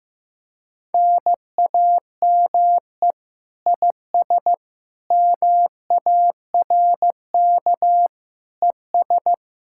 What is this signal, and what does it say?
Morse code, heard as a keyed tone, about 15 wpm, 700 Hz. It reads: NAME IS MARK ES
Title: Speed-Racing Common QSO Phrases Encore - 15wpm